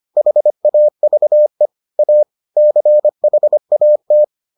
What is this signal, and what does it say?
Morse code, heard as a keyed tone, about 25 wpm, 600 Hz. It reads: HAVE A CHAT